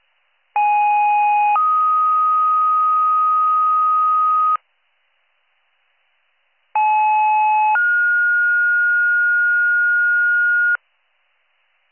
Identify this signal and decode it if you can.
Two-tone page, heard as a one-second tone, then a three-second tone, about 830 then 1250 Hz, about 860 then 1430 Hz. It is heard from 0.6 to 10.8 s.